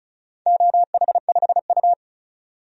Morse code, heard as a keyed tone, about 36 wpm, 700 Hz. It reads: OH5U
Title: DXCC-100 Call Signs - 4 Characters - 35wpm